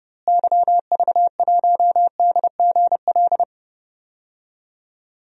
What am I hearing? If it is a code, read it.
Morse code, heard as a keyed tone, about 31 wpm, 700 Hz. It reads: YV1DGL
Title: DXCC-100 Call Signs - 6 Characters - Repeated 3x - 30wpm